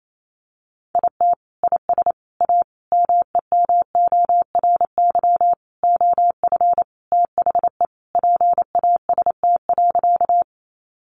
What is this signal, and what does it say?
Morse code, heard as a keyed tone, about 28 wpm, 700 Hz. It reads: IT IS A MEMORY OF THE PAST.
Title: Sentences from Top 1300 Words 28wpm